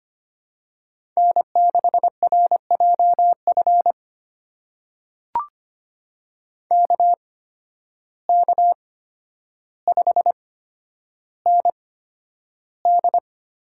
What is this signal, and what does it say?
Morse code, heard as a keyed tone, about 25 wpm, spaced as for 11 wpm, 700 Hz. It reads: N6RJF KK5ND
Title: USA Call Signs - 5 Characters - 8x Character Spacing - 25wpm